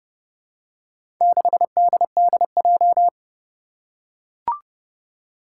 Morse code, heard as a keyed tone, about 30 wpm, 700 Hz. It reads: 6DDJ
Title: Four Challenging Letter-Number Combinations 30wpm